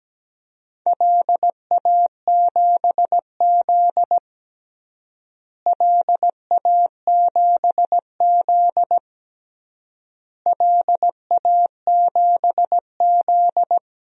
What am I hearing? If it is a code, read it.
Morse code, heard as a keyed tone, about 17 wpm, 700 Hz. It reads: LA7Z LA7Z LA7Z
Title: DXCC-100 Call Signs - 4 Characters - Repeated 3x - 17wpm